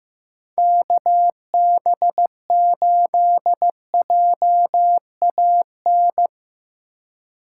Morse code, heard as a keyed tone, about 17 wpm, 700 Hz. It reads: KB8JAN
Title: Speed Racing - USA 2x3 Call Signs 15wpm